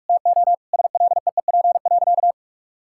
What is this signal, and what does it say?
Morse code, heard as a keyed tone, about 46 wpm, 700 Hz. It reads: TO SLEEP.
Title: Sentences from Top 1000 Words - Review Part 6 - 45wpm